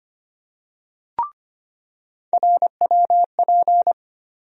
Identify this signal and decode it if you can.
Morse code, heard as a keyed tone, about 25 wpm, 700 Hz. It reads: RWP
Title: USA Call Sign Postfixes - 3 Letters 25wpm